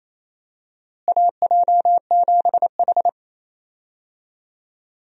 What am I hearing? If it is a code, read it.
Morse code, heard as a keyed tone, about 28 wpm, 700 Hz. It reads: AJ7H
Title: USA Call Signs - 4 Characters - Repeated 2x - 28wpm